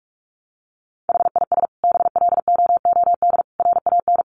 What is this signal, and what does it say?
Morse code, heard as a keyed tone, about 44 wpm, 700 Hz. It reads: HIS BLOOD RAN
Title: Sentences from Top 1100 Words 45wpm